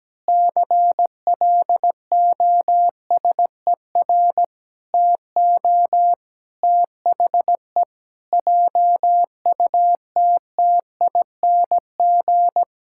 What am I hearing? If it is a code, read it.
Morse code, heard as a keyed tone, about 17 wpm, 700 Hz. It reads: CLOSER TO THE JUTTING
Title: Pangram Challenge v2 - 17wpm